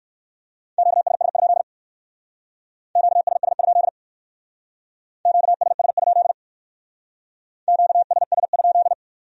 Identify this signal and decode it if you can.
Morse code, heard as a keyed tone, about 55 wpm, 700 Hz. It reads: QSS? QSS? QSS? QSS?
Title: Speed-Racing Q-Codes - 45wpm